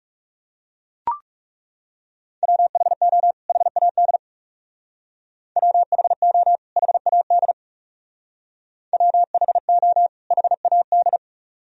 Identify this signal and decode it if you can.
Morse code, heard as a keyed tone, about 38 wpm, 700 Hz. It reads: WHO HAD WHO HAD WHO HAD